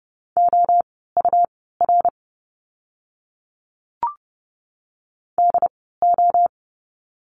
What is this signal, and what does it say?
Morse code, heard as a keyed tone, about 30 wpm, spaced as for 17 wpm, 700 Hz. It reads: OUR DO
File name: course / 30:10wpm / 237 Lesson 38f - Letter q - Words 30wpm